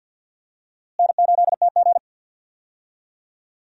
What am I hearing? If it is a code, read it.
Morse code, heard as a keyed tone, about 46 wpm, 700 Hz. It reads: N8TG
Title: USA Call Signs - 4 Characters - Repeated 3x - 50wpm